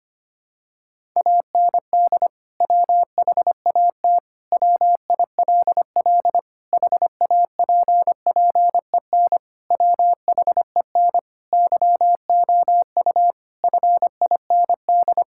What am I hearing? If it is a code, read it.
Morse code, heard as a keyed tone, about 25 wpm, 700 Hz. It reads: AND WHAT WILL HAPPEN WHEN YOU FIND
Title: Sentences from Top 1000 Words - Review Part 5 - 25wpm